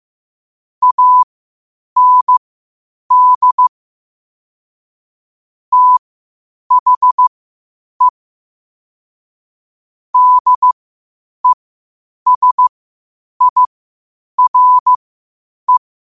Morse code, heard as a keyed tone, about 15 wpm, spaced as for 8 wpm, 1000 Hz. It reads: AND THE DESIRE
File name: qotdfiles / 8wpm